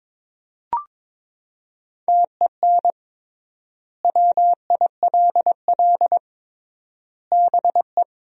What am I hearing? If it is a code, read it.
Morse code, heard as a keyed tone, about 22 wpm, 700 Hz. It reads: TEN WILL BE
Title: 3x Word-Spacing - Sentences from Top 400 Words - 22wpm